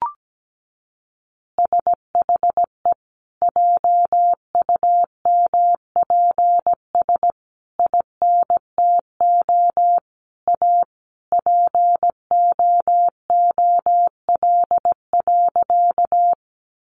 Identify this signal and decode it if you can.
Morse code, heard as a keyed tone, about 17 wpm, 700 Hz. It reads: SHE JUMPS INTO A POOL.
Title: Additional Easy Phrases 17wpm